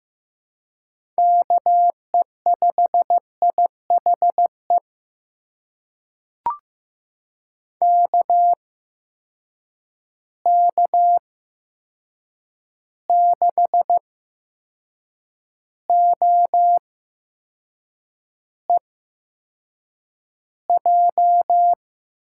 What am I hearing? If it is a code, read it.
Morse code, heard as a keyed tone, about 15 wpm, spaced as for 6 wpm, 700 Hz. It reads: KE5IHE KK6OEJ
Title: USA Call Signs - 6 Characters - 8x Character Spacing - 15wpm